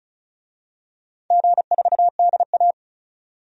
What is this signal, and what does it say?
Morse code, heard as a keyed tone, about 36 wpm, 700 Hz. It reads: G4DA
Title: DXCC-100 Call Signs - 4 Characters - 35wpm